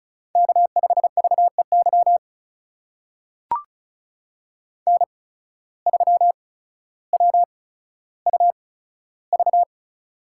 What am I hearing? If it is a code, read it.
Morse code, heard as a keyed tone, about 36 wpm, spaced as for 15 wpm, 700 Hz. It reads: K5VEY N3WUV
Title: USA Call Signs - 5 Characters - 8x Character Spacing - 35wpm